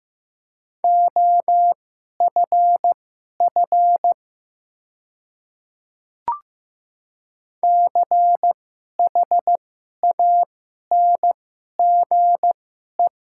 Morse code, heard as a keyed tone, about 15 wpm, spaced as for 11 wpm, 700 Hz. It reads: OFF CHANGE
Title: 2x Character Spacing - Top 300 Words - 15wpm